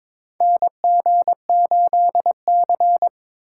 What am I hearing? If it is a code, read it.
Morse code, heard as a keyed tone, about 22 wpm, 700 Hz. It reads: NG8C
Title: USA Call Signs - 4 Characters - 4x Character Spacing - 22wpm